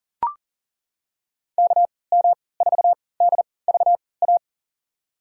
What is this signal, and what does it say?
Morse code, heard as a keyed tone, about 37 wpm, spaced as for 22 wpm, 700 Hz. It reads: KM4DVA
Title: USA Call Signs - 6 Characters - 3x Character Spacing - 40wpm